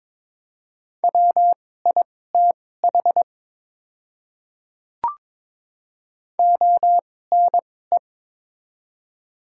Morse code, heard as a keyed tone, about 22 wpm, spaced as for 16 wpm, 700 Hz. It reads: WITH ONE
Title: 2x Character Spacing - Top 300 Words - 22wpm